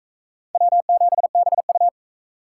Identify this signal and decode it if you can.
Morse code, heard as a keyed tone, about 44 wpm, 700 Hz. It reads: W7BU